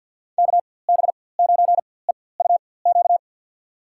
Morse code, heard as a keyed tone, about 49 wpm, spaced as for 23 wpm, 700 Hz. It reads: KB9EUQ
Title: USA Call Signs - 6 Characters - 4x Character Spacing - 50wpm